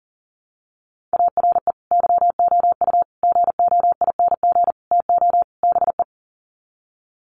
Morse code, heard as a keyed tone, about 40 wpm, 700 Hz. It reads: ARE YOU GOING TO BE
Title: Sets of 5 Words Encore 40wpm